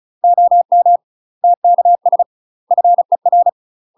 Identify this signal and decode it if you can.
Morse code, heard as a keyed tone, about 37 wpm, 700 Hz. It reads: OM TKS FER
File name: UK MASTER QSOs TRAINING / 35WPM / UK MASTER QSOs 0001